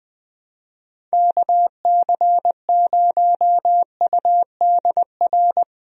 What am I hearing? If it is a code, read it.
Morse code, heard as a keyed tone, about 21 wpm, 700 Hz. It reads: KC0UDR
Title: USA Call Signs - 6 Characters - Repeated 2x - 20wpm